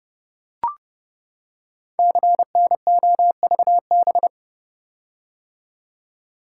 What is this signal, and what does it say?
Morse code, heard as a keyed tone, about 30 wpm, 700 Hz. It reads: CNOVB
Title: Code Groups - 5 Letters - 30wpm